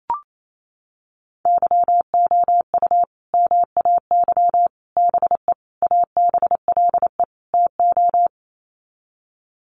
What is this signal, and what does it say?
Morse code, heard as a keyed tone, about 28 wpm, 700 Hz. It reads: YOU MAY BE ABLE TO
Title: Sets of 5 Words Encore 28wpm